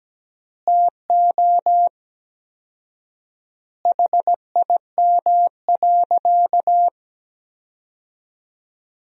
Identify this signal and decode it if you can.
Morse code, heard as a keyed tone, about 17 wpm, 700 Hz. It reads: TO HIM.
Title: 4x Word-Spacing - Sentences from Top 400 Words - 17wpm